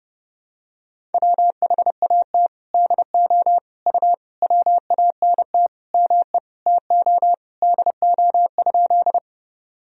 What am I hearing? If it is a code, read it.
Morse code, heard as a keyed tone, about 30 wpm, 700 Hz. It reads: WHAT DO U WANT ME TO DO?
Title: Sentences from Top 200 Words and Abbreviations 30wpm